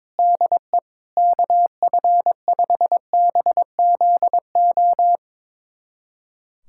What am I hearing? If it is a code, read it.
Morse code, heard as a keyed tone, about 22 wpm, 700 Hz. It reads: DE KF5BZO
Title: USA Callsign DE Callsign 22wpm